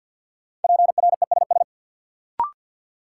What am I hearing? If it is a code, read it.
Morse code, heard as a keyed tone, about 51 wpm, 700 Hz. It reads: PRESS
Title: Sentences from Top 800 Words - New Word Drill - 50wpm